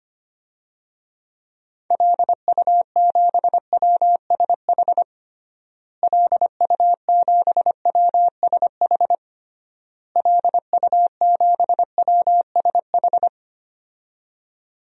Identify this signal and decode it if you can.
Morse code, heard as a keyed tone, about 25 wpm, 700 Hz. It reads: LU7WSH LU7WSH LU7WSH
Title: DXCC-100 Call Signs - 6 Characters - Repeated 3x - 25wpm